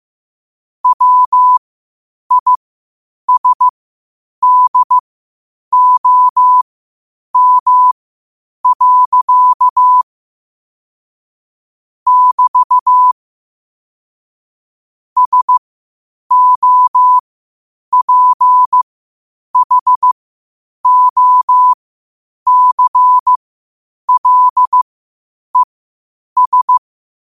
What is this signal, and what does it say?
Morse code, heard as a keyed tone, about 15 wpm, spaced as for 8 wpm, 1000 Hz. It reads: WISDOM. <BT> SOPHOCLES
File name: qotdfiles / 8wpm